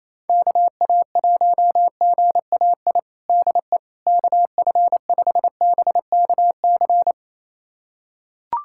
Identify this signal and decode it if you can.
Morse code, heard as a keyed tone, about 28 wpm, 700 Hz. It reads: KA1GAI DE KF5BKC
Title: USA Callsign DE Callsign - 6 Characters - 28wpm